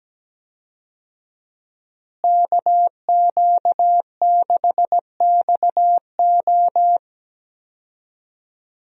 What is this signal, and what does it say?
Morse code, heard as a keyed tone, about 17 wpm, 700 Hz. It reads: KQ6XO
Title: USA Call Signs - 5 Characters - Repeated 2x - 17wpm